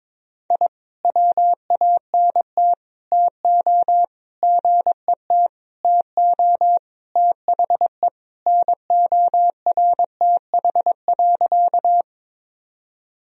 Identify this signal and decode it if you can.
Morse code, heard as a keyed tone, about 22 wpm, 700 Hz. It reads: I WANT TO GET TO THE NORTH.
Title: Sentences from Top 1000 Words - Review Part 1 - 22wpm